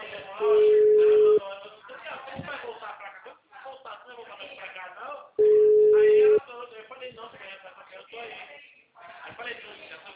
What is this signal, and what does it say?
Ringback tone (Europe, 425 Hz), with speech.